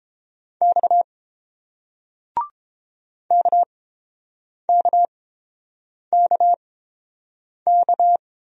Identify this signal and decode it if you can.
Morse code, heard as a keyed tone, about 31 wpm, spaced as for 7 wpm, 690 Hz. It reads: X KKKK